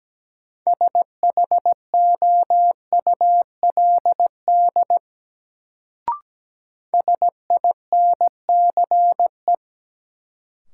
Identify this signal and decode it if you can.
Morse code, heard as a keyed tone, about 17 wpm, 700 Hz. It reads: SHOULD SINCE
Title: Sets of 2 Words - New Word Drill - 17wpm